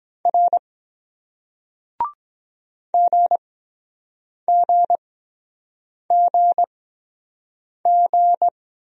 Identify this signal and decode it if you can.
Morse code, heard as a keyed tone, about 23 wpm, spaced as for 7 wpm, 700 Hz. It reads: R GGGG